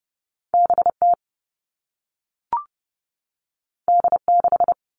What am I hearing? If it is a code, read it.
Morse code, heard as a keyed tone, about 30 wpm, 700 Hz. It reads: BT D6
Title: All Cut Numbers - 2-Digit 30wpm